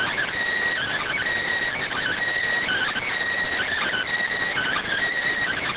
Emergency Alert System (SAME) tones (header bursts), heard as harsh buzzing bursts. It is heard from 0.8 to 5.0 s.